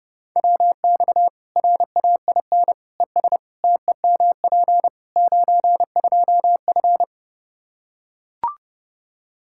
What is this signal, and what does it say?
Morse code, heard as a keyed tone, about 32 wpm, 700 Hz. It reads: WX RAIN ES TEMP 92F
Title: Practice QSO - Exchange 2 - Info - 30wpm